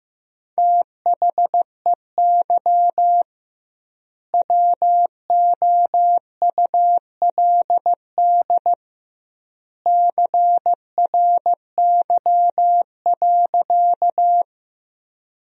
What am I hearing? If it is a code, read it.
Morse code, heard as a keyed tone, about 16 wpm, 700 Hz. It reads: THEY WOULD CRY.
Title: 2x Word-Spacing - Sentences from Top 500 Words - 15wpm